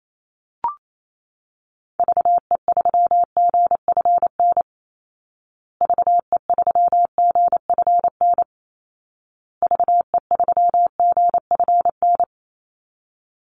Morse code, heard as a keyed tone, about 28 wpm, 700 Hz. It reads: VE3GFN VE3GFN VE3GFN
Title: Call Signs Repeated 2x - 28wpm